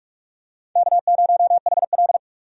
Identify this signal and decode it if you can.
Morse code, heard as a keyed tone, about 43 wpm, 700 Hz. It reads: K0HL
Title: USA Call Signs - 4 Characters - 45wpm